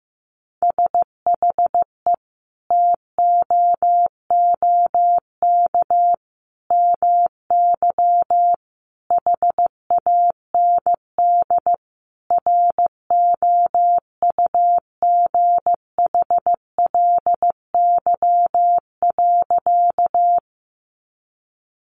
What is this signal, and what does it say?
Morse code, heard as a keyed tone, about 15 wpm, 700 Hz. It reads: SHE TOOK MY HAND ROUGHLY.
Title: Sentences from Top 1500 Words 15wpm